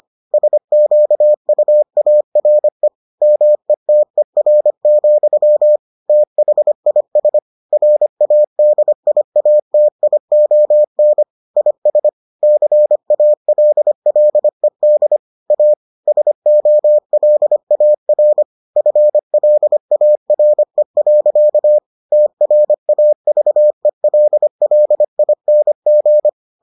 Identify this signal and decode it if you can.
Morse code, heard as a keyed tone, about 25 wpm, 600 Hz. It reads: SQUARE METER, THIS RADIATION IS CALLED A SOLAR FLARE. TRAVELLING